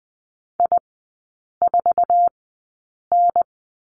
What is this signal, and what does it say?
Morse code, heard as a keyed tone, about 20 wpm, spaced as for 8 wpm, 700 Hz. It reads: I4N
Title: 047 Lesson 9e - Number 4 - Three Characters 20wpm